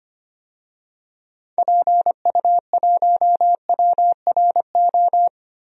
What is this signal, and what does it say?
Morse code, heard as a keyed tone, about 25 wpm, 700 Hz. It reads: PU1WRO
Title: DXCC-100 Call Signs - 6 Characters - 25wpm